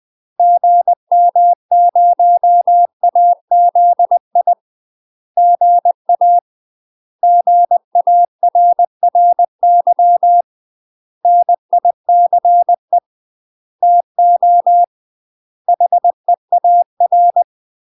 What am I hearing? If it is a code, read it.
Morse code, heard as a keyed tone, about 20 wpm, 700 Hz. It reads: GM0AZI GA GARRY NICE TO HEAR
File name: UK MASTER QSOs TRAINING / 20WPM / UK MASTER QSOs 0004